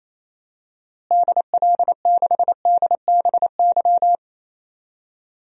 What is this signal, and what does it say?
Morse code, heard as a keyed tone, about 28 wpm, 700 Hz. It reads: DL6DBY
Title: DXCC-100 Call Signs - 6 Characters - Repeated 3x - 28wpm